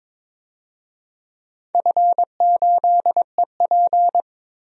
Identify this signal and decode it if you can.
Morse code, heard as a keyed tone, about 22 wpm, 700 Hz. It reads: F8EP
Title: DXCC-100 Call Signs - 4 Characters - 22wpm